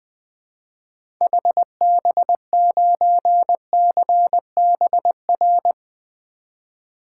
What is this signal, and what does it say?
Morse code, heard as a keyed tone, about 21 wpm, 700 Hz. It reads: HB9CBR
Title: 2023 SOTA Activator Call Signs - 20wpm